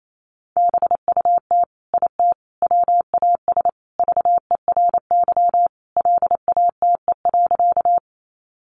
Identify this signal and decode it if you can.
Morse code, heard as a keyed tone, about 28 wpm, 700 Hz. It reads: BUT IT WAS VERY LATE.
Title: Sentences from Top 300 Words 28wpm